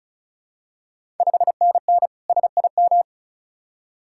Morse code, heard as a keyed tone, about 35 wpm, 690 Hz. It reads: 5NN SIM
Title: Ontario QSO Party Exchange 35wpm